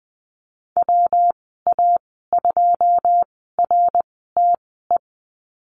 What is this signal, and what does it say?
Morse code, heard as a keyed tone, about 20 wpm, spaced as for 15 wpm, 700 Hz. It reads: WA2RTE
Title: 140 Lesson 23g - Letter f - Call Signs 20wpm